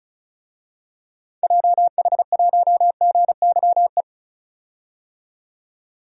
Morse code, heard as a keyed tone, about 34 wpm, 700 Hz. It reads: JH1GYE
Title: DXCC-100 Call Signs - 6 Characters - 35wpm